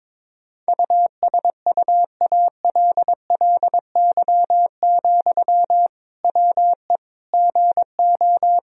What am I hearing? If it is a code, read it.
Morse code, heard as a keyed tone, about 21 wpm, 700 Hz. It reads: USUALLY, WE GO
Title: Sentences from Top 1000 Words - Review Part 3 - 22wpm